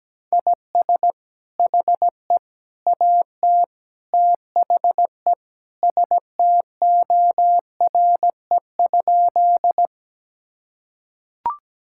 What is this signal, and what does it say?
Morse code, heard as a keyed tone, about 17 wpm, 700 Hz. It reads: IS HE AT THE STORE?
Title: Sentences from Top 1000 Words - Review Part 5 - 17wpm